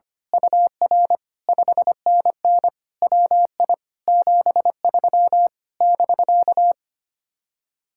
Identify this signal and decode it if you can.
Morse code, heard as a keyed tone, about 24 wpm, 700 Hz. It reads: UR 5NN WI 73 <BK>
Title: Speed-Racing Common QSO Phrases - 25wpm